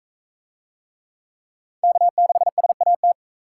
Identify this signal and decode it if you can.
Morse code, heard as a keyed tone, about 40 wpm, 700 Hz. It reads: K6SAT